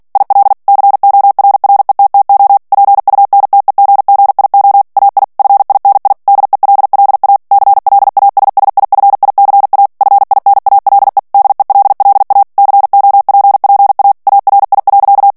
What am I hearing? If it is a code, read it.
Morse code, heard as a keyed tone, about 48 wpm, 800 Hz. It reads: IL CORRETTO PUNTEGGIO AI FINI DELLA CLASSIFICA FINALE DELLA COPPA ARI.
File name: ITALIEN texte clair 2 vitesse 45 mots minute